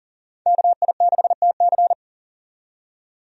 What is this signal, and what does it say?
Morse code, heard as a keyed tone, about 40 wpm, 700 Hz. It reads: KI6TC
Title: USA Call Signs - 5 Characters - 40wpm